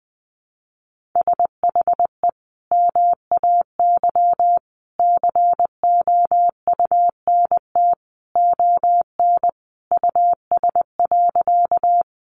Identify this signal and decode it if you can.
Morse code, heard as a keyed tone, about 19 wpm, 700 Hz. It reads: SHE MAY COUNT ON US.
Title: Sentences from Top 1500 Words 20wpm